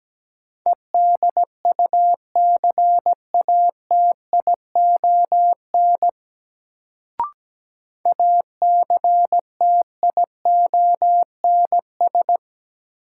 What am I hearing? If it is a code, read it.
Morse code, heard as a keyed tone, about 17 wpm, 700 Hz. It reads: EDUCATION ACTIONS
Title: Sentences from Top 1000 Words - Review Part 6 - New Word Drill - 17wpm